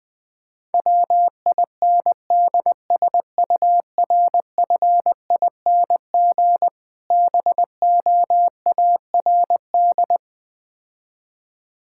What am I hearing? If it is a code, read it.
Morse code, heard as a keyed tone, about 20 wpm, 700 Hz. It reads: WINDSURFING BOARD
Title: Difficult Words - Speed-Racing with Hints - 20wpm